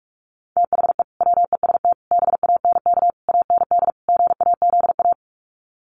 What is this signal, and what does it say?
Morse code, heard as a keyed tone, about 44 wpm, 700 Hz. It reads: THE WEST BANK AND GAZA
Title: Sets of 5 Words 45wpm